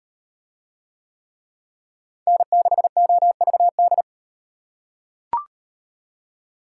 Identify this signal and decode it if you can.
Morse code, heard as a keyed tone, about 38 wpm, 700 Hz. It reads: N6OVD